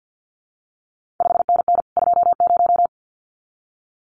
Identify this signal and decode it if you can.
Morse code, heard as a keyed tone, about 50 wpm, 700 Hz. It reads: 5NN 20
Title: CQ WW DX Contest Exchange - 50wpm